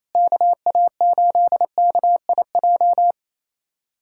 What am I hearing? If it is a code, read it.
Morse code, heard as a keyed tone, about 28 wpm, 700 Hz. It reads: KA8KIJ
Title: USA Callsign DE Callsign - 6 Characters - 28wpm